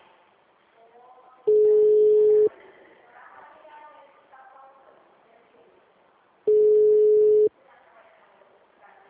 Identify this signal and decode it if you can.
Ringback tone (Europe, 425 Hz), with speech.